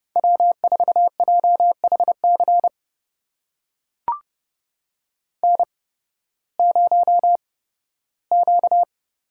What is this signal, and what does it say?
Morse code, heard as a keyed tone, about 32 wpm, spaced as for 16 wpm, 700 Hz. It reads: W4JHC N0Q
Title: USA Call Signs - 5 Characters - 8x Character Spacing - 30wpm